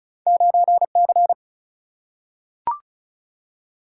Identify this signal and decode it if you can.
Morse code, heard as a keyed tone, about 36 wpm, 700 Hz. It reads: 9C
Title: Single-Letter Single-Number 35wpm